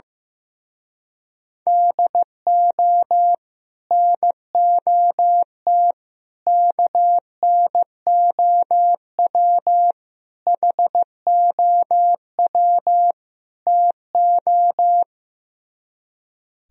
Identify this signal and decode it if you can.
Morse code, heard as a keyed tone, about 16 wpm, 700 Hz. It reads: DO NOT KNOW HOW TO
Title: Speed-Racing - Sets of 5 Words 15wpm